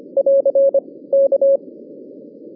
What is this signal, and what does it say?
Morse code, heard as a keyed tone, about 25 wpm, 550 Hz. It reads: <AR> K